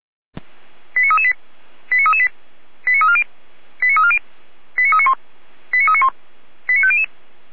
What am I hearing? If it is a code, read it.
Five-tone selective call (ZVEI), heard as quick runs of five tones, tones of about 70 ms. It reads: 89338 89338 89340 89340 89391 89391 89500